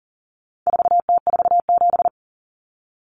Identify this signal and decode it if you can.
Morse code, heard as a keyed tone, about 40 wpm, 700 Hz. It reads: 4T47
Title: Common Cut Numbers - 4-Digit 40wpm